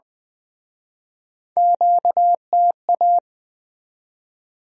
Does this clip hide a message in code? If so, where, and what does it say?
Yes: Morse code from 1.6 to 3.2 s, heard as a keyed tone, about 22 wpm, 700 Hz. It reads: QTA